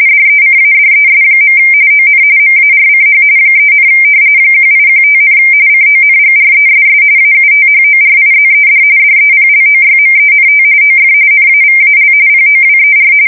RTTY, about 45.45 baud. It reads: DECADES. ON MARCH 3, 1815, CONGRESS AUTHORIZED DEPLOYMENT OF NAVAL POWER
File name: fsk170 / usnavy2